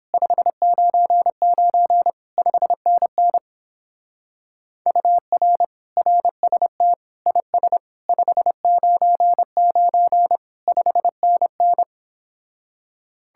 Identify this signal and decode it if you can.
Morse code, heard as a keyed tone, about 29 wpm, 700 Hz. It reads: 599 5NN UR RST IS 599 5NN